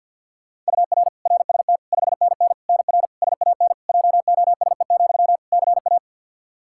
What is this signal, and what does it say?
Morse code, heard as a keyed tone, about 47 wpm, 700 Hz. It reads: UR RST 5NN NR SAN JOSE, CA
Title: Practice QSO - Exchange 1 - Beginning - 50wpm